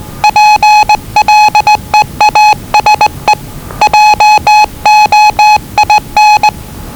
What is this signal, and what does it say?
Morse code, heard as a keyed tone, about 20 wpm, 850 Hz. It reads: PLEASE JOIN
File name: WellingtonRadioZlwCloseDownOnHfAndMf / 01-Zlw 4277khz 417.5 qrt until 0925z